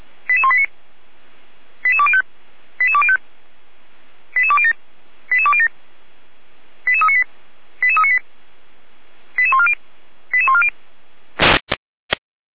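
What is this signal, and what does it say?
Five-tone selective call (ZVEI), heard as quick runs of five tones, tones of about 70 ms. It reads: 88189 88205 88205 88207 88207 88398 88398 88140 88140
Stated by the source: ZVEI five-tone selective call received with an SDR.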